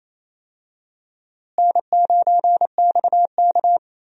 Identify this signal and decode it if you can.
Morse code, heard as a keyed tone, about 28 wpm, 700 Hz. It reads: N9XK
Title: USA Call Signs - 4 Characters - Repeated 3x - 28wpm